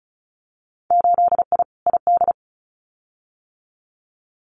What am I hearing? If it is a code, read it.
Morse code, heard as a keyed tone, about 36 wpm, 700 Hz. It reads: 8I ID
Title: Winter Field Day Exchanges Encore 35wpm